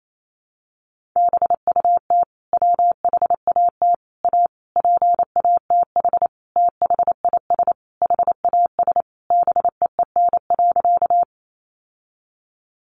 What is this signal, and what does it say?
Morse code, heard as a keyed tone, about 28 wpm, 700 Hz. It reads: BUT WHAT A PATH THIS HAS BEEN.
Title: Sentences from Top 1400 Words 28wpm